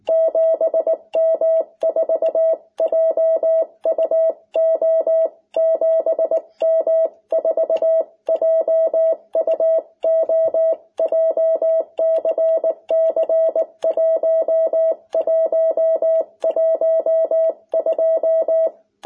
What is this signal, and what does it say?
Morse code, heard as a keyed tone, about 19 wpm, 660 Hz. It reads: 7M4JUO 7M4JUO JCC 1112
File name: jh8cbh / 2021cw